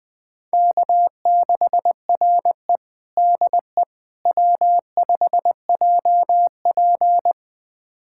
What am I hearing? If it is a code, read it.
Morse code, heard as a keyed tone, about 20 wpm, 700 Hz. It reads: K6RE DE W5JP
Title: USA Callsign DE Callsign - 4 Characters - 20wpm